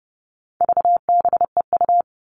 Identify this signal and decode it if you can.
Morse code, heard as a keyed tone, about 30 wpm, 700 Hz. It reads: VBEU
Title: All Cut Numbers - 4-Digit 30wpm